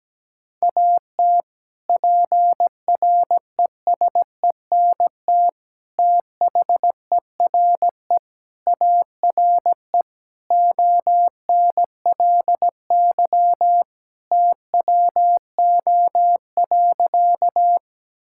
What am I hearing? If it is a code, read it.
Morse code, heard as a keyed tone, about 17 wpm, 700 Hz. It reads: AT PRESENT THERE ARE ONLY TWO.
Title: Sentences from Top 1000 Words - Review Part 5 - 17wpm